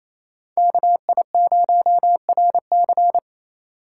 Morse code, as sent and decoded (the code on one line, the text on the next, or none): -.- .. ----- .-. -.-.
KI0RC